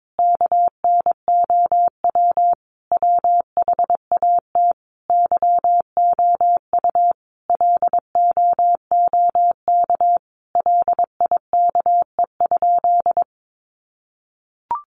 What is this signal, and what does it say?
Morse code, heard as a keyed tone, about 22 wpm, 700 Hz. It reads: KNOW WHAT YOU LOOK LIKE?
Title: Sentences from Top 100 Words 22wpm